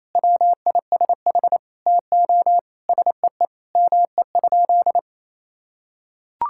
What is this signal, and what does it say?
Morse code, heard as a keyed tone, about 28 wpm, 700 Hz. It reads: WISH TO SEE ME?
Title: Sentences from Top 1600 Words 28wpm